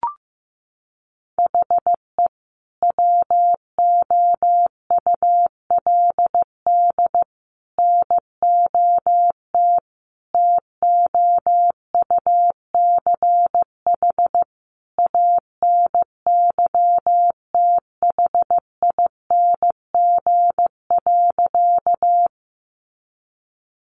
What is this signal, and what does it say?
Morse code, heard as a keyed tone, about 15 wpm, 700 Hz. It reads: HE WOULD NOT TOUCH ANYTHING.
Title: Sentences from Top 1200 Words 15wpm